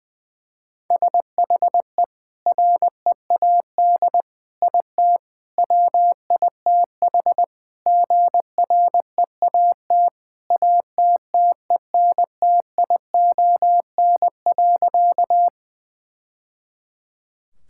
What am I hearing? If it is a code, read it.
Morse code, heard as a keyed tone, about 20 wpm, 700 Hz. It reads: SHE READ IT WITH GREAT ATTENTION.
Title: Sentences from Top 1000 Words - Review Part 6 - 20wpm